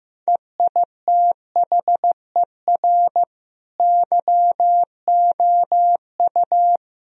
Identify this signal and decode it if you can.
Morse code, heard as a keyed tone, about 15 wpm, 700 Hz. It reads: EITHER YOU
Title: Sentences from Top 1000 Words - Review Part 4 - 15wpm